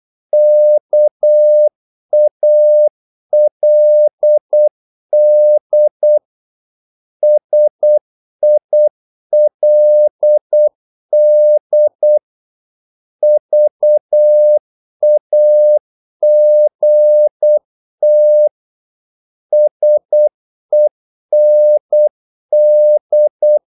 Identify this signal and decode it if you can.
Morse code, heard as a keyed tone, about 8 wpm, 600 Hz. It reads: KALD SILD VAGT SEND
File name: Korte ord DK 8wpm